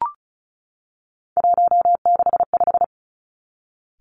Morse code, heard as a keyed tone, about 36 wpm, 700 Hz. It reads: 165
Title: Three Challenging Letter-Number Combinations 35wpm